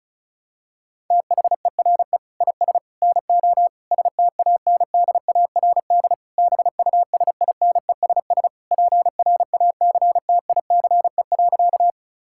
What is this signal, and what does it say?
Morse code, heard as a keyed tone, about 36 wpm, 700 Hz. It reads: THERE IS NO STANDARD BUSINESS PRACTICE.
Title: Sentences from Top 1000 Words - Review Part 6 - 35wpm